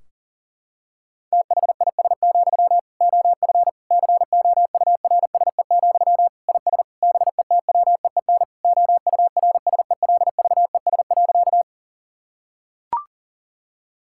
Morse code, heard as a keyed tone, about 40 wpm, 700 Hz. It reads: THIS, OF COURSE, IS BETWEEN OURSELVES.
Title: Sentences from Top 1600 Words 40wpm